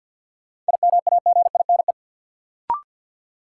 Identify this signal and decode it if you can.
Morse code, heard as a keyed tone, about 53 wpm, 700 Hz. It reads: IMAGINE
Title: Sentences from Top 1400 Words - New Word Drill - 50wpm